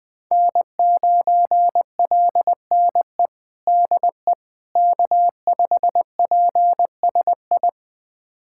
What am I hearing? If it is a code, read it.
Morse code, heard as a keyed tone, about 20 wpm, 700 Hz. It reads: N9LNE DE K5PSI
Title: USA Callsign DE Callsign - 5 Characters - 20wpm